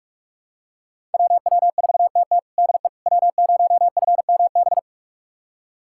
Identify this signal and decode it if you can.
Morse code, heard as a keyed tone, about 42 wpm, 700 Hz. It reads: WW4TT DE W0FMB
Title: USA Callsign DE Callsign - 5 Characters - 45wpm